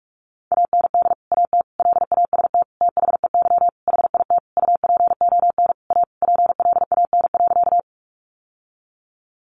Morse code, heard as a keyed tone, about 44 wpm, 700 Hz. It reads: AND AT LAST THEY HIT UPON A PLAN.
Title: Sentences from Top 1200 Words 45wpm